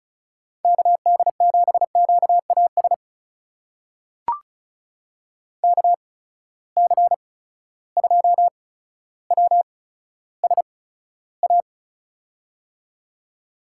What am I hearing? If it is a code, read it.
Morse code, heard as a keyed tone, about 36 wpm, spaced as for 15 wpm, 700 Hz. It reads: KD7QAS KC2WSA
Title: USA Call Signs - 6 Characters - 8x Character Spacing - 35wpm